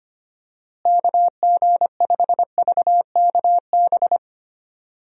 Morse code, heard as a keyed tone, about 25 wpm, 700 Hz. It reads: KG5VKB
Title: USA Call Signs - 6 Characters - Repeated 3x - 25wpm